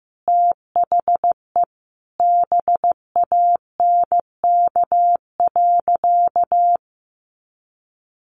Morse code, heard as a keyed tone, about 15 wpm, 700 Hz. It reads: THE BANK.
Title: Sentences from Top 1400 Words 15wpm